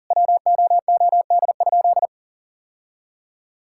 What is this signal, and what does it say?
Morse code, heard as a keyed tone, about 40 wpm, 700 Hz. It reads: WOOD?
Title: Sentences from Top 1600 Words 40wpm